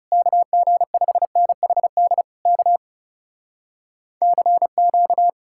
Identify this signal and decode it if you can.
Morse code, heard as a keyed tone, about 40 wpm, 700 Hz. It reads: KG5NHD K CQ